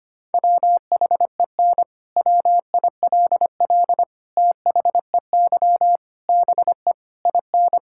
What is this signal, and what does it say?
Morse code, heard as a keyed tone, about 25 wpm, 700 Hz. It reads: WHEN WILL THEY BE IN
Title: Sentences from Top 1000 Words - Review Part 6 - 25wpm